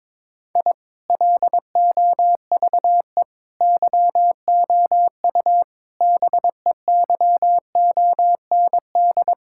Morse code, heard as a keyed tone, about 21 wpm, 700 Hz. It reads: I LOVE YOU BEYOND
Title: Sentences from Top 1000 Words - Review Part 6 - 22wpm